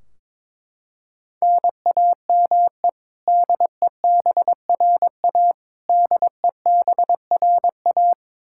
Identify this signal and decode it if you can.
Morse code, heard as a keyed tone, about 22 wpm, 700 Hz. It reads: NAME DEBRA DEBRA
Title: Practice QSO - Exchange 1 - Beginning - 22wpm